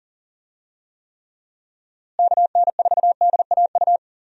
Morse code, heard as a keyed tone, about 40 wpm, 700 Hz. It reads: KN4DAU
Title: USA Call Signs - 6 Characters - Repeated 2x - 40wpm